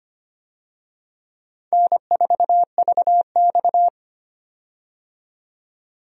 Morse code, heard as a keyed tone, about 25 wpm, 700 Hz. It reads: N4VX
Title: USA Call Signs - 4 Characters - 25wpm